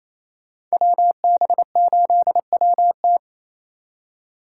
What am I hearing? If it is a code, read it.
Morse code, heard as a keyed tone, about 35 wpm, 700 Hz. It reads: WB8WT
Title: Speed Racing - USA 2x2 Call Signs 28wpm